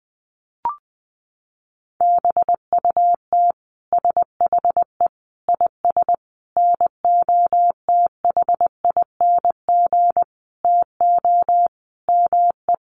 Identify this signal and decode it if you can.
Morse code, heard as a keyed tone, about 20 wpm, 700 Hz. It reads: BUT SHE IS NOTHING TO ME
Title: Sentences from Top 500 Words 20wpm